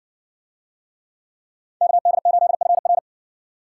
Morse code, heard as a keyed tone, about 55 wpm, 700 Hz. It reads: KD8FR